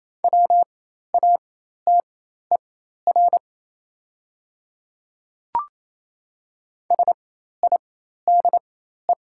Morse code, heard as a keyed tone, about 28 wpm, spaced as for 13 wpm, 700 Hz. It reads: WATER SIDE
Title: 4x Character Spacing - Top 100 Words - 28wpm